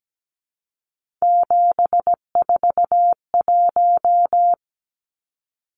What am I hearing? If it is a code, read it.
Morse code, heard as a keyed tone, about 17 wpm, 700 Hz. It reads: 741
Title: Numbers - 3-Digit 17wpm